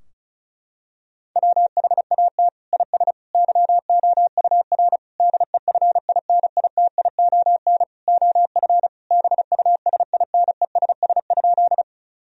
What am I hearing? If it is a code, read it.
Morse code, heard as a keyed tone, about 36 wpm, 700 Hz. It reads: WHAT IS YOUR DEFINITION OF BUSINESS?
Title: Sentences from Top 1000 Words - Review Part 4 - 35wpm